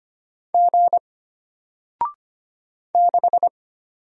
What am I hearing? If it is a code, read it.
Morse code, heard as a keyed tone, about 25 wpm, 700 Hz. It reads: G 6
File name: Single-Letter-Number-25wpm